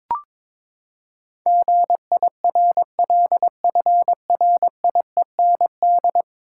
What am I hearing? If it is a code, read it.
Morse code, heard as a keyed tone, about 21 wpm, 700 Hz. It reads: GIRLFRIEND
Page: morse-code-ninja-practice-22wpm